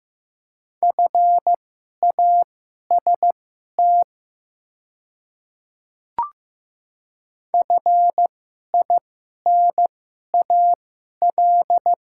Morse code, heard as a keyed tone, about 15 wpm, spaced as for 11 wpm, 690 Hz. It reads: FAST FINAL
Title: 2x Character Spacing - Top 500 Words - 15wpm